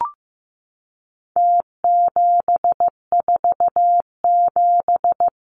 Morse code, heard as a keyed tone, about 15 wpm, 700 Hz. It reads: T747
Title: Numbers - Contest Style - 4-Digit 15wpm